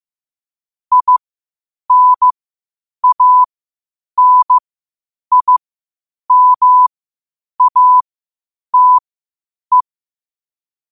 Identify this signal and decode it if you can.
Morse code, heard as a keyed tone, about 15 wpm, spaced as for 8 wpm, 1000 Hz. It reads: INANIMATE
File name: qotdfiles / 8wpm